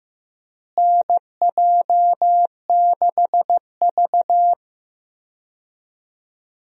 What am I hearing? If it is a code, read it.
Morse code, heard as a keyed tone, about 15 wpm, 700 Hz. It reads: NJ6V
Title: USA Call Signs - 4 Characters - Repeated 2x - 15wpm